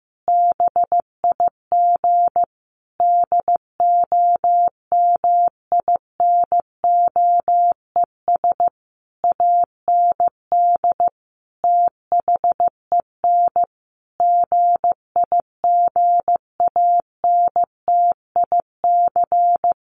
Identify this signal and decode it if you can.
Morse code, heard as a keyed tone, about 15 wpm, 700 Hz. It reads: BIG DOMINOES AND THEN GIGANTIC
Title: A Sound Of Thunder 15wpm